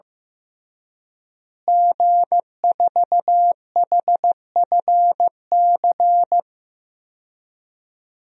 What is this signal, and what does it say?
Morse code, heard as a keyed tone, about 15 wpm, 700 Hz. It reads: G4HFC
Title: DXCC-100 Call Signs - 5 Characters - Repeated 3x - 15wpm